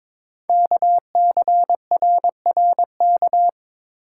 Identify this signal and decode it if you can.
Morse code, heard as a keyed tone, about 23 wpm, 700 Hz. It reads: KCRRK
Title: Code Groups - 5 Letter:Numbers - 22wpm